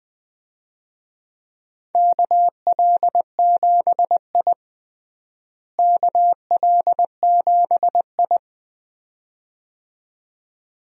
Morse code, heard as a keyed tone, about 21 wpm, 700 Hz. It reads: KL7I KL7I
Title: DXCC-100 Call Signs - 4 Characters - Repeated 2x - 20wpm